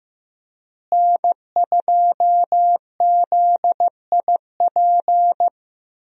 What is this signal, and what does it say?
Morse code, heard as a keyed tone, about 15 wpm, 700 Hz. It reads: N2ZIP
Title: USA Call Signs - 5 Characters - Repeated 3x - 15wpm